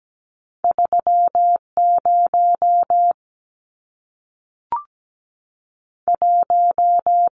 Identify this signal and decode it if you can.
Morse code, heard as a keyed tone, about 17 wpm, 700 Hz. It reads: 3 0 1
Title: Numbers - 2-Digit 17wpm